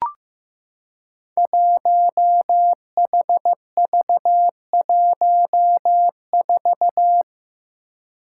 Morse code, heard as a keyed tone, about 15 wpm, 700 Hz. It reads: 1HV14
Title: Five Challenging Letter-Number Combinations 15wpm